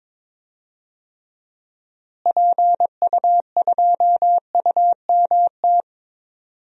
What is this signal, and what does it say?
Morse code, heard as a keyed tone, about 22 wpm, 700 Hz. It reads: PU2UMT
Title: DXCC-100 Call Signs - 6 Characters - Repeated 3x - 22wpm